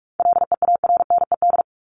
Morse code, heard as a keyed tone, about 44 wpm, 700 Hz. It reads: LEARNED
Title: ED Suffix - Top 100 Words with Base Words 45wpm